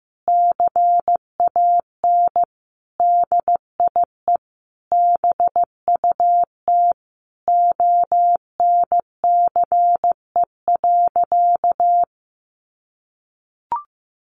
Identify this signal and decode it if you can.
Morse code, heard as a keyed tone, about 15 wpm, 700 Hz. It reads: CAN DIE BUT ONCE.
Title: Sentences from Top 1300 Words 15wpm